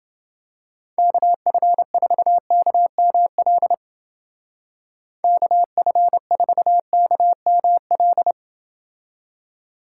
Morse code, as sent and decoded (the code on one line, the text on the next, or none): -.- ..-. ....- -.- -- .-.. / -.- ..-. ....- -.- -- .-..
KF4KML KF4KML